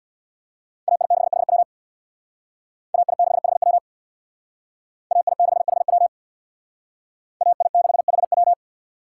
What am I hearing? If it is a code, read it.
Morse code, heard as a keyed tone, about 60 wpm, 700 Hz. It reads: AI6HW AI6HW AI6HW AI6HW
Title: Speed Racing - USA 2x2 Call Signs 50wpm